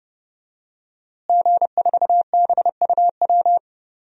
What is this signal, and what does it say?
Morse code, heard as a keyed tone, about 30 wpm, 700 Hz. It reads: G4BUW
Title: DXCC-100 Call Signs - 5 Characters - Repeated 3x - 30wpm